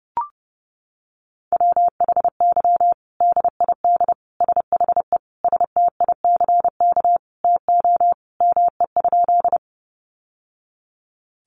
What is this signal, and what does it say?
Morse code, heard as a keyed tone, about 30 wpm, 700 Hz. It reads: WHY DID SHE STICK TO ME?
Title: Sentences from Top 1200 Words 30wpm